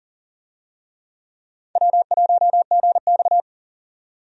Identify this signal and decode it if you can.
Morse code, heard as a keyed tone, about 37 wpm, 700 Hz. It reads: W1GX
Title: USA Call Signs - 4 Characters - 40wpm